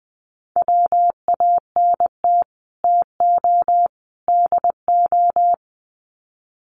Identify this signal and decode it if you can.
Morse code, heard as a keyed tone, about 20 wpm, 700 Hz. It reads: WANT TO DO
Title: Sets of 3 Words 20wpm